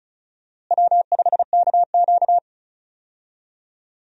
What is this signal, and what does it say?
Morse code, heard as a keyed tone, about 36 wpm, 700 Hz. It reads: W5KQ
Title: USA Call Signs - 4 Characters - Repeated 3x - 35wpm